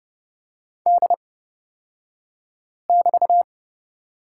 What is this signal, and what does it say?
Morse code, heard as a keyed tone, about 29 wpm, spaced as for 5 wpm, 690 Hz. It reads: D<BT>
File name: Lesson 36d - BT - Two Characters 30wpm